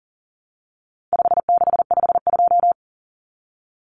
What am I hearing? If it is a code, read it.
Morse code, heard as a keyed tone, about 40 wpm, 700 Hz. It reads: 5652
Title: Common Cut Numbers - 4-Digit 40wpm